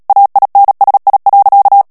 Morse code, heard as a keyed tone, about 37 wpm, 800 Hz. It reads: AINSI.
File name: CW / FICHIERS-MP3 / FRANCAIS TEXTE CLAIR 2 VITESSE 35 MOTS MINUTES